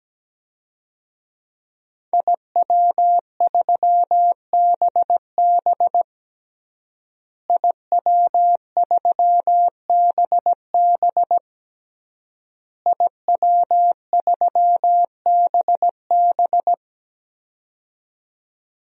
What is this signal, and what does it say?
Morse code, heard as a keyed tone, about 17 wpm, 700 Hz. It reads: IW3BB IW3BB IW3BB
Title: DXCC-100 Call Signs - 5 Characters - Repeated 3x - 17wpm